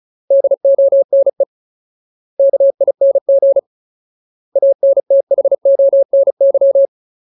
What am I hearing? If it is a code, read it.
Morse code, heard as a keyed tone, about 34 wpm, 550 Hz. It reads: DONE KING ANTHONY
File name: Ultimate Word List / 35WPM Ultimate